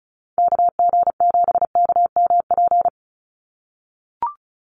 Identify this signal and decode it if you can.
Morse code, heard as a keyed tone, about 34 wpm, 700 Hz. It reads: KG7KMP
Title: USA Call Signs 35wpm